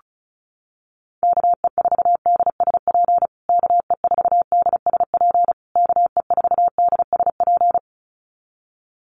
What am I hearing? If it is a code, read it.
Morse code, heard as a keyed tone, about 34 wpm, 700 Hz. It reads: KE4DSP KE4DSP KE4DSP
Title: USA Call Signs Repeated 3x 35wpm